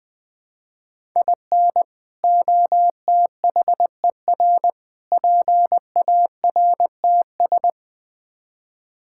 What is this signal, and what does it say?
Morse code, heard as a keyed tone, about 20 wpm, 700 Hz. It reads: IN OTHER PARTS
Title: Build A Phrase - 5-Word Phrases - 20wpm